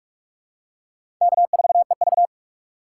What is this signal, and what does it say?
Morse code, heard as a keyed tone, about 46 wpm, 700 Hz. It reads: K4EV
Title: USA Call Signs - 3x Character Spacing - 45wpm